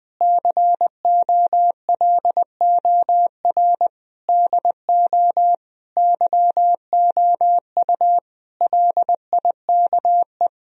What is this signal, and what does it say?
Morse code, heard as a keyed tone, about 20 wpm, 700 Hz. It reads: COLOR DO YOU LIKE
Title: Sentences from Top 1000 Words - Review Part 2 - 20wpm